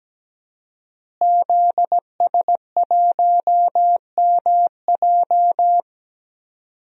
Morse code, heard as a keyed tone, about 17 wpm, 700 Hz. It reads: ZS1MJ
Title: DXCC-100 Call Signs - 5 Characters - Repeated 2x - 17wpm